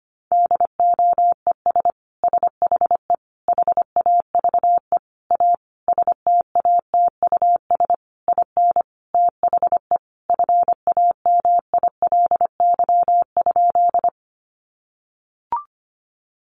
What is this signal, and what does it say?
Morse code, heard as a keyed tone, about 25 wpm, 700 Hz. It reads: DOES SHE HAVE A STATUS IN THE FAMILY?
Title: Sentences from Top 1400 Words 25wpm